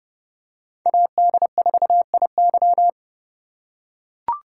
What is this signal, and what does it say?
Morse code, heard as a keyed tone, about 29 wpm, 700 Hz. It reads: AD4IY